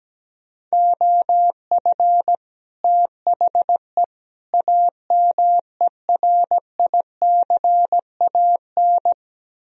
Morse code, heard as a keyed tone, about 17 wpm, 700 Hz. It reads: OF THE AMERICAN
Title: Speed-Racing - Sets of 3 Words 17wpm